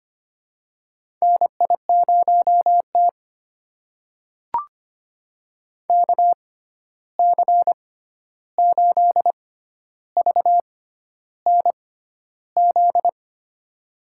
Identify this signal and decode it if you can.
Morse code, heard as a keyed tone, about 25 wpm, spaced as for 11 wpm, 700 Hz. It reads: NI0T KC8VNZ
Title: USA Call Signs - 6x Character Spacing - 25wpm